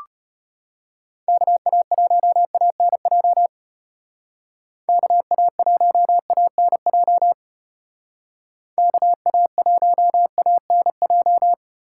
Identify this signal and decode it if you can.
Morse code, heard as a keyed tone, about 32 wpm, 700 Hz. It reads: KA1ANJ KA1ANJ KA1ANJ